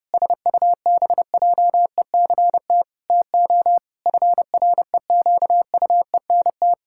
Morse code, heard as a keyed tone, about 30 wpm, 700 Hz. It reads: SUBJECT TO FREQUENT
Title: Pangram Challenge v2 - 30wpm